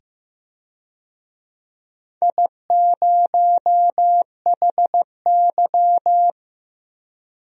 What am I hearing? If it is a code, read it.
Morse code, heard as a keyed tone, about 15 wpm, 700 Hz. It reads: I0HY
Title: DXCC-100 Call Signs - 4 Characters - 15wpm